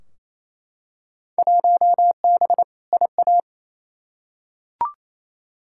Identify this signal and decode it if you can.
Morse code, heard as a keyed tone, about 28 wpm, 700 Hz. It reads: 1B IA
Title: Field Day Exchanges Encore - 28wpm